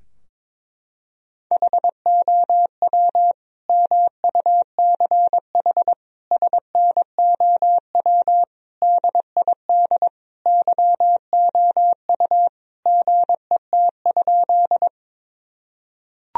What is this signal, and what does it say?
Morse code, heard as a keyed tone, about 22 wpm, 700 Hz. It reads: HOW MUCH SNOW DID YOU GET?
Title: Sentences from Top 1000 Words - Review Part 3 - 22wpm